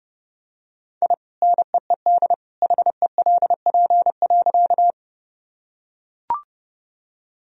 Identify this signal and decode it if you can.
Morse code, heard as a keyed tone, about 30 wpm, 700 Hz. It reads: I NEED HELP.
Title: Sentences from Top 1000 Words - Review Part 1 - 30wpm